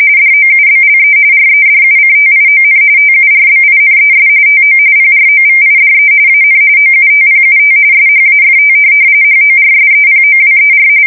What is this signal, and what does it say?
RTTY, about 45.45 baud. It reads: DOUBLE REDUCTION TURBINES TO PROVIDE AN OUTPUT SHAFT HORSEPOWER
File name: fsk170 / USS-New-Jersey